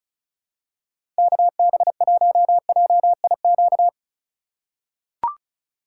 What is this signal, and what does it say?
Morse code, heard as a keyed tone, about 36 wpm, 700 Hz. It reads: KB1JIQ
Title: USA Call Signs - 6 Characters - 3x Character Spacing - 35wpm